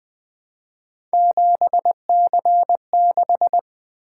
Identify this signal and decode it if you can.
Morse code, heard as a keyed tone, about 20 wpm, 700 Hz. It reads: 7C6
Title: Three Letter-Number Combinations - 20wpm